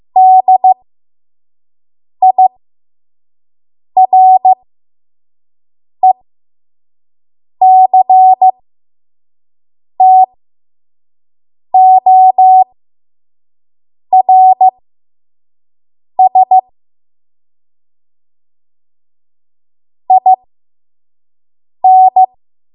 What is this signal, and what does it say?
Morse code, heard as a keyed tone, about 15 wpm, spaced as for 5 wpm, 750 Hz. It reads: DIRECTORS IN
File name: Morse / 05WPM